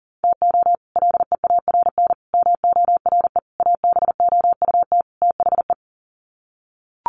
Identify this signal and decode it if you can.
Morse code, heard as a keyed tone, about 39 wpm, 700 Hz. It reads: TO LEARN MORE ABOUT THE
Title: Sets of 5 Words Encore2 40wpm